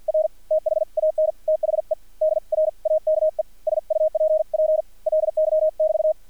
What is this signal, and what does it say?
Morse code, heard as a keyed tone, about 50 wpm, spaced as for 41 wpm, 630 Hz, with radio noise. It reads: R THAN THE DRAKE SUPPLY<BT>
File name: archived / CRSnet